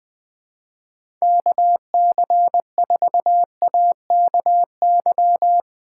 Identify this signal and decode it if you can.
Morse code, heard as a keyed tone, about 22 wpm, 700 Hz. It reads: KC4AKY
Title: Speed Racing - USA 2x3 Call Signs 20wpm